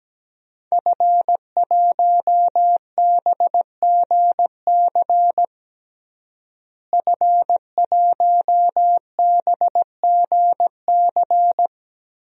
Morse code, heard as a keyed tone, about 17 wpm, 700 Hz. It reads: F1BGC F1BGC
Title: DXCC-100 Call Signs - 5 Characters - Repeated 3x - 17wpm